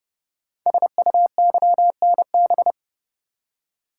Morse code, heard as a keyed tone, about 30 wpm, 700 Hz. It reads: SUYNB
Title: Code Groups - 5 Letters - 30wpm